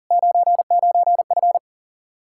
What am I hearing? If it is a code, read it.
Morse code, heard as a keyed tone, about 48 wpm, 700 Hz. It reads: 99F
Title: Speed-Racing Common QSO Phrases Encore - 40wpm